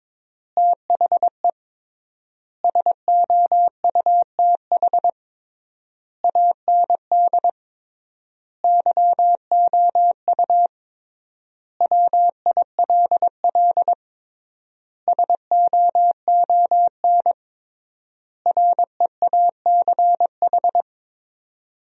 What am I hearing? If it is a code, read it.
Morse code, heard as a keyed tone, about 22 wpm, 700 Hz. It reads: THE SOUTH AND YOU WILL SOON REACH
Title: 3x Word-Spacing - Sentences from Top 400 Words - 22wpm